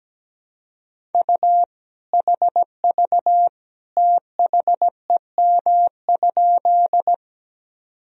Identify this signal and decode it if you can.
Morse code, heard as a keyed tone, about 17 wpm, 700 Hz. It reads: U HV THEM?
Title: Sentences from Top 100 Words and Abbreviations 17wpm